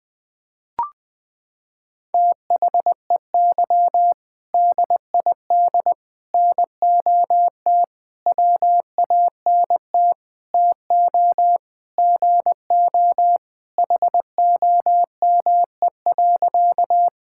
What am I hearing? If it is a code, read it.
Morse code, heard as a keyed tone, about 21 wpm, 700 Hz. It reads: THEY DID NOT WANT TO GO HOME.
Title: Sentences with 5-Word Phrases - 20wpm